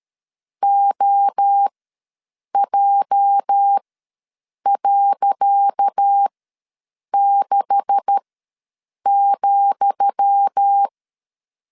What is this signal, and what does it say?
Morse code, heard as a keyed tone, about 13 wpm, spaced as for 7 wpm, 800 Hz. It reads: OJ.6,
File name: LEZIONI / 10 SIGNS / 10D COMA ws 12wpm 3618